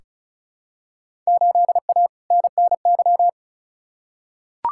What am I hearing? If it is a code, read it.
Morse code, heard as a keyed tone, about 36 wpm, 690 Hz. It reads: 8A NNY
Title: Field Day Exchanges Encore - 35wpm